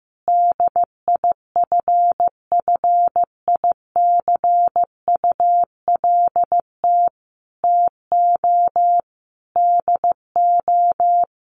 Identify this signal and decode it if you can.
Morse code, heard as a keyed tone, about 15 wpm, 700 Hz. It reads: DIFFICULT TO DO
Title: Sentences from Top 600 Words 15wpm